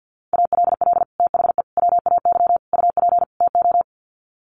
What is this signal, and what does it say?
Morse code, heard as a keyed tone, about 50 wpm, 700 Hz. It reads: ALL THE WAY UP TO
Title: Sets of 5 Words Encore 50wpm